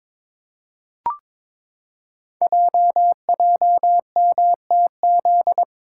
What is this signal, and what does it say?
Morse code, heard as a keyed tone, about 22 wpm, 700 Hz. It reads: JJMTZ
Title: Code Groups - 5 Letters - 22wpm